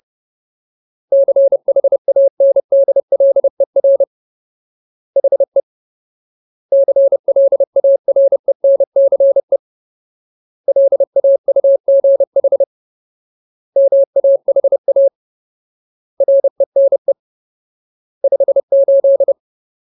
Morse code, heard as a keyed tone, about 30 wpm, 550 Hz. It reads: CHANDLER HE CLARENCE LAUGH MAHA RENE 58
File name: Ultimate Word List / 30WPM Ultimate